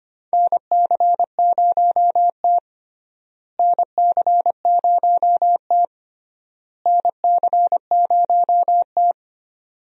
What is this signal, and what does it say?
Morse code, heard as a keyed tone, about 25 wpm, 700 Hz. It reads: NC0T NC0T NC0T
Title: USA Call Signs - 4 Characters - Repeated 3x - 25wpm